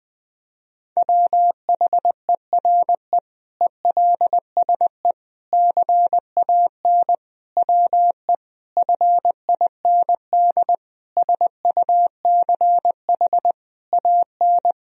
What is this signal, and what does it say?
Morse code, heard as a keyed tone, about 20 wpm, 700 Hz. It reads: WHERE ELSE CAN WE FIND SUCH AN
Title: Sentences from Top 1000 Words - Review Part 1 - 20wpm